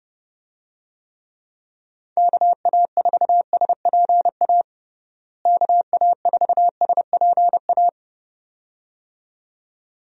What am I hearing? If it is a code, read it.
Morse code, heard as a keyed tone, about 30 wpm, 700 Hz. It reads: KA4SPA KA4SPA
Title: USA Call Signs - 6 Characters - Repeated 2x - 30wpm